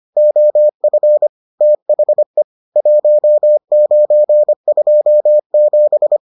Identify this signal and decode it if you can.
Morse code, heard as a keyed tone, about 25 wpm, 600 Hz. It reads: OF THE 1927